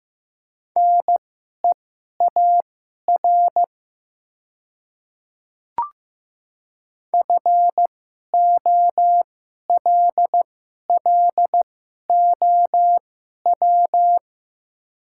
Morse code, heard as a keyed tone, about 15 wpm, spaced as for 11 wpm, 700 Hz. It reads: NEAR FOLLOW
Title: 2x Character Spacing - Top 300 Words - 15wpm